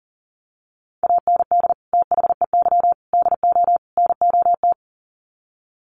Morse code, heard as a keyed tone, about 40 wpm, 700 Hz. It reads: AND THEY DO NOT
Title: Sets of 4 Words 40wpm